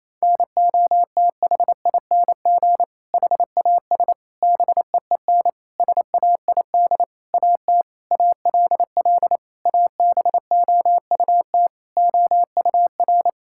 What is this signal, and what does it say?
Morse code, heard as a keyed tone, about 28 wpm, 700 Hz. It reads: NOTHING HAS BEEN SAID AT ALL ABOUT OUR
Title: Sentences from Top 1000 Words - Review Part 5 - 28wpm